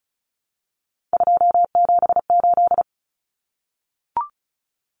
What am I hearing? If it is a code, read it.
Morse code, heard as a keyed tone, about 34 wpm, 700 Hz. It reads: 278
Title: Common Cut Numbers - 3-Digit 35wpm